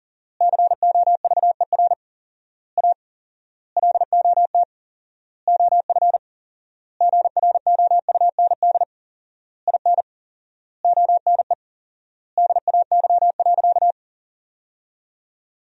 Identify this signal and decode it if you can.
Morse code, heard as a keyed tone, about 40 wpm, 700 Hz. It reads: COVER A LOT OF GROUND IN ONE DAY.
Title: 4x Word-Spacing - Sentences from Top 500 Words - 40wpm